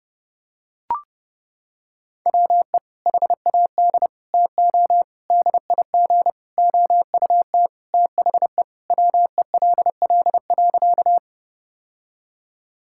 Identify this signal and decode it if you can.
Morse code, heard as a keyed tone, about 32 wpm, 700 Hz. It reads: WE HAD TO DIG OUT THE WELL.
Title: Sentences from Top 1600 Words 30wpm